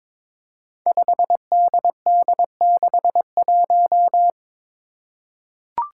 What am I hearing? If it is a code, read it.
Morse code, heard as a keyed tone, about 22 wpm, 700 Hz. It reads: 5DD61
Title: Five Challenging Letter-Number Combinations 22wpm